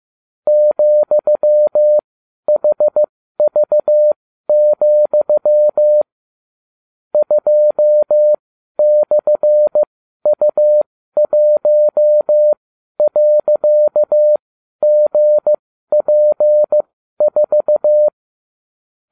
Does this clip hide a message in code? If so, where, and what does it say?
Yes: Morse code from 0.5 to 18.1 s, heard as a keyed tone, about 15 wpm, spaced as for 12 wpm, 600 Hz. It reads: ,HV, 2/U1.GP4